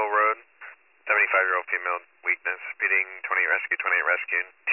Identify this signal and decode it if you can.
Emergency Alert System (SAME) tones, heard as harsh buzzing bursts, with speech. It is heard from 1.1 to 3.6 s.